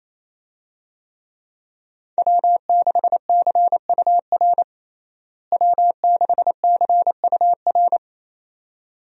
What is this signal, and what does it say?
Morse code, heard as a keyed tone, about 28 wpm, 700 Hz. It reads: W6CUR W6CUR
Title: USA Call Signs - 5 Characters - Repeated 2x - 28wpm